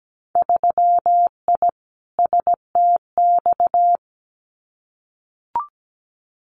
Morse code, heard as a keyed tone, about 17 wpm, 700 Hz. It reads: 3I STX
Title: Winter Field Day Exchanges Encore 17wpm